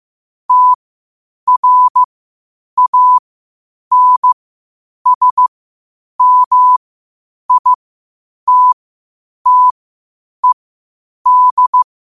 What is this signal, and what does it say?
Morse code, heard as a keyed tone, about 15 wpm, spaced as for 8 wpm, 1000 Hz. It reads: TRANSMITTED
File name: qotdfiles / 8wpm